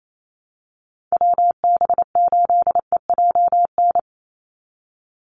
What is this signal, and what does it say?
Morse code, heard as a keyed tone, about 28 wpm, 700 Hz. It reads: WB8EJN
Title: Call Signs - 28wpm